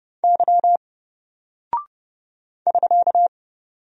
Morse code, heard as a keyed tone, about 30 wpm, 700 Hz. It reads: Y <SK>
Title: Single Character (Letter:Number:Punctuation:Prosign) 30wpm